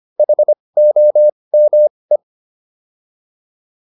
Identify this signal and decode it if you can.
Morse code, heard as a keyed tone, about 25 wpm, spaced as for 20 wpm, 600 Hz. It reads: HOME
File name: PR301 20